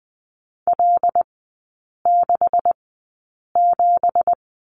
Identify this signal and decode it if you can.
Morse code, heard as a keyed tone, about 20 wpm, spaced as for 8 wpm, 700 Hz. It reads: L67
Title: 166 Lesson 27e - Number 7 - Three Characters 20wpm